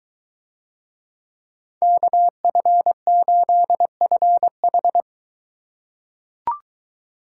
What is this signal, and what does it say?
Morse code, heard as a keyed tone, about 22 wpm, 700 Hz. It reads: KF8FH